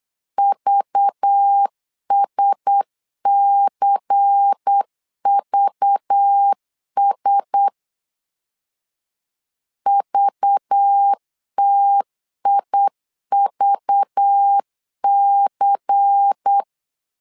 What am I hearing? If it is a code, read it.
Morse code, heard as a keyed tone, about 8 wpm, 800 Hz. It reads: VSCVS VTIVC
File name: LEZIONI / 5 LBVC / 5 LBVC 8wpm 2935